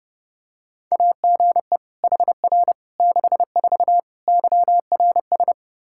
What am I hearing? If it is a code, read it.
Morse code, heard as a keyed tone, about 30 wpm, 700 Hz. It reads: AGE HR 64 YRS
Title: K1USN SST QSO Practice 30wpm